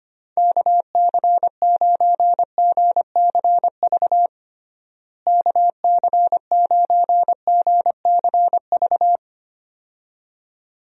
Morse code, heard as a keyed tone, about 25 wpm, 700 Hz. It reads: KC9GCV KC9GCV
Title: USA Call Signs - 6 Characters - Repeated 2x - 25wpm